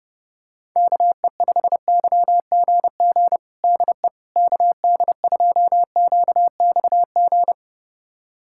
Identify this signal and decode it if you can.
Morse code, heard as a keyed tone, about 30 wpm, 700 Hz. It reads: KE5YGG DE KD2QXG
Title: USA Callsign DE Callsign - 6 Characters - 30wpm